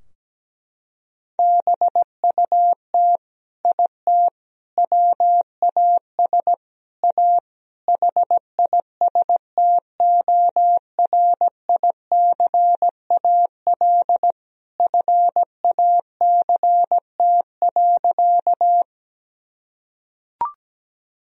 Morse code, heard as a keyed tone, about 17 wpm, 700 Hz. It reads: BUT IT WAS A HISTORICAL FACT.
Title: Sentences from Top 1000 Words - Review Part 4 - 17wpm